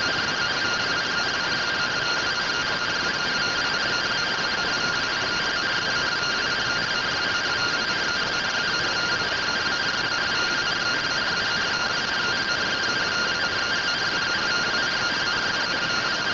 Two-tone teleprinter signal, about 170 Hz shift.